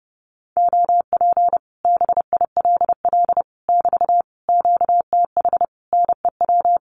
Morse code, heard as a keyed tone, about 30 wpm, 700 Hz. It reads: OP BILL <BT> QTH NEW
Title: Common QSO Phrases Encore 30wpm